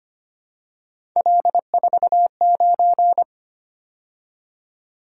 Morse code, heard as a keyed tone, about 25 wpm, 700 Hz. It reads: L49
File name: Three Letter-Number Combinations - 25wpm